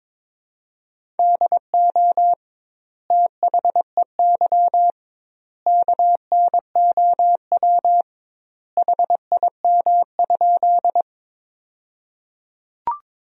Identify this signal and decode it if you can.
Morse code, heard as a keyed tone, about 23 wpm, 700 Hz. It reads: DO THEY KNOW HIM?
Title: 2x Word-Spacing - Sentences from Top 100 Words - 22wpm